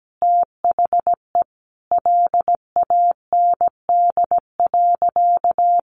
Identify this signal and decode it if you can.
Morse code, heard as a keyed tone, about 17 wpm, 700 Hz. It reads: THE LAND.
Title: Sentences from Top 1500 Words 17wpm